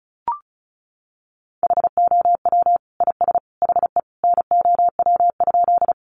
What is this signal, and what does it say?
Morse code, heard as a keyed tone, about 34 wpm, 700 Hz. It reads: HOW IS HE NOW?
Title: Sentences from Top 100 Words 35wpm